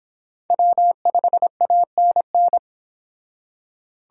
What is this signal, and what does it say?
Morse code, heard as a keyed tone, about 25 wpm, 700 Hz. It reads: W5ANN